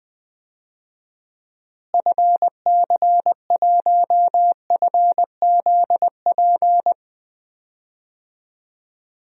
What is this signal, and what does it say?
Morse code, heard as a keyed tone, about 20 wpm, 700 Hz. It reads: FC1FZP
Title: DXCC-100 Call Signs - 6 Characters - Repeated 3x - 20wpm